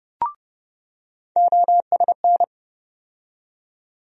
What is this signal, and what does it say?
Morse code, heard as a keyed tone, about 30 wpm, 700 Hz. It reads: OSN
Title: USA Call Sign Postfixes - 3 Letters 30wpm